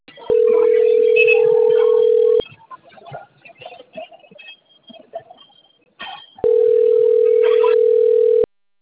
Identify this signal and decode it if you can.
Ringback tone (North America), 440 + 480 Hz.